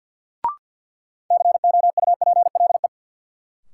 Morse code, heard as a keyed tone, about 53 wpm, 700 Hz. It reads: COUPLE
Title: Sets of 2 Words - New Word Drill - 50wpm